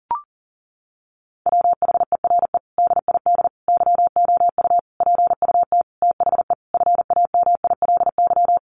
Morse code, heard as a keyed tone, about 37 wpm, 700 Hz. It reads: WHERE DID YOU PUT THE FAMILY
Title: Sentences from Top 1200 Words 40wpm